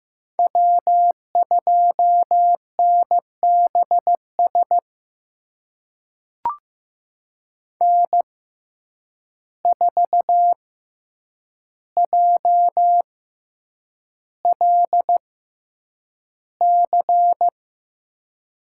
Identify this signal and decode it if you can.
Morse code, heard as a keyed tone, about 15 wpm, spaced as for 8 wpm, 700 Hz. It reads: W2NBS N4JLC
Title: USA Call Signs - 6x Character Spacing - 15wpm